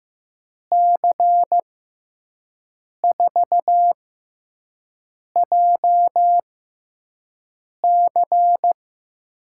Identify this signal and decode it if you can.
Morse code, heard as a keyed tone, about 15 wpm, spaced as for 5 wpm, 700 Hz. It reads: C4JC
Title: USA Call Signs - 6x Character Spacing - 15wpm